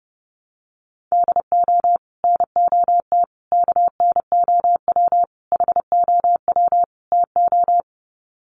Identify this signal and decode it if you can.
Morse code, heard as a keyed tone, about 30 wpm, 700 Hz. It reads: DO NOT KNOW HOW TO
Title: Sets of 5 Words 30wpm